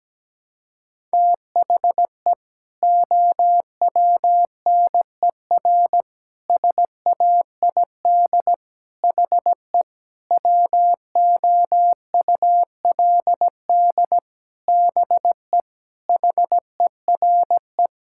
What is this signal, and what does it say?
Morse code, heard as a keyed tone, about 17 wpm, 700 Hz. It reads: THE OWNER SAID HE WOULD BE HERE
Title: Sentences from Top 1600 Words 17wpm